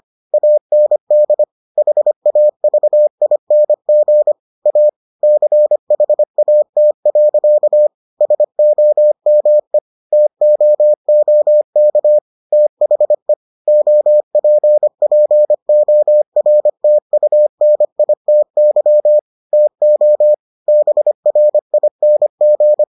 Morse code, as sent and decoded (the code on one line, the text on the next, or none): .- -. -.. / .... .- ...- .. -. --. / .- / -.-. .... .- - .-.-.- / ... --- -- . / - --- --- -.- / - .... / --- .--. .--. --- .-. - ..- -. .. - -.-- / - --- / -... .-. .. -. --.
AND HAVING A CHAT. SOME TOOK THE OPPORTUNITY TO BRING